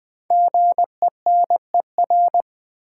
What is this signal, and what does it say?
Morse code, heard as a keyed tone, about 20 wpm, 700 Hz. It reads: GENER
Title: Build A Word - 5 Letters At A Time - 10-Letter Words - 20wpm